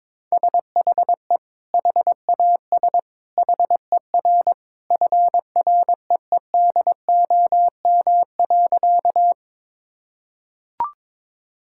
Morse code, heard as a keyed tone, about 22 wpm, 700 Hz. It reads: SHE HAS HER FREEDOM.
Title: Sentences from Top 1000 Words - Review Part 5 - 22wpm